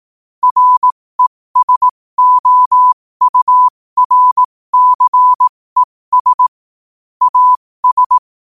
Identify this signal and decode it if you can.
Morse code, heard as a keyed tone, about 18 wpm, spaced as for 15 wpm, 1000 Hz. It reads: RESOURCES AS
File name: qotdfiles / 15wpm